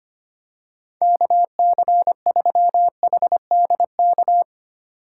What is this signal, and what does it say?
Morse code, heard as a keyed tone, about 25 wpm, 700 Hz. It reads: KC3HDK
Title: USA Call Signs - 6 Characters - 2x Character Spacing - 25wpm